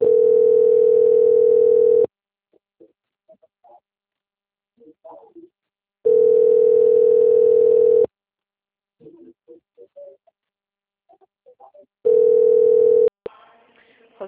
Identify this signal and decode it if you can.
Ringback tone (North America), 440 + 480 Hz.